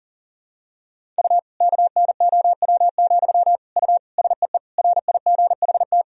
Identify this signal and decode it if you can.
Morse code, heard as a keyed tone, about 40 wpm, 700 Hz. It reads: U KNOW, U SEE RIGHT
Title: Sentences from Top 200 Words and Abbreviations 40wpm